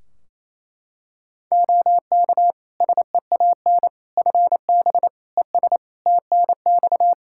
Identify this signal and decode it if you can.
Morse code, heard as a keyed tone, about 27 wpm, 700 Hz. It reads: OK SEAN FB ES TNX
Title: Practice QSO - Exchange 2 - Info - 28wpm